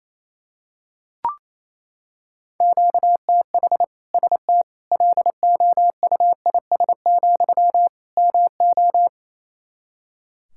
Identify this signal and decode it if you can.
Morse code, heard as a keyed tone, about 28 wpm, 700 Hz. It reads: QTH ST LOUIS, MO
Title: Practice QSO - Exchange 1 - Beginning - 28wpm